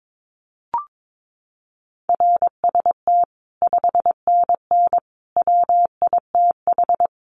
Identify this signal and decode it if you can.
Morse code, heard as a keyed tone, about 22 wpm, 700 Hz. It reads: RST 5NN WITH
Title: Common QSO Phrases 22wpm